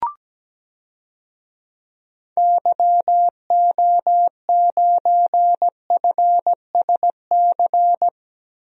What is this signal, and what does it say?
Morse code, heard as a keyed tone, about 17 wpm, 700 Hz. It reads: YO9FSC
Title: DXCC-100 Call Signs - 6 Characters - Repeated 2x - 17wpm